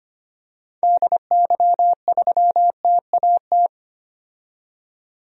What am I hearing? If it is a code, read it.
Morse code, heard as a keyed tone, about 25 wpm, 700 Hz. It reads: DY3TAT
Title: DXCC-100 Call Signs - 6 Characters - 25wpm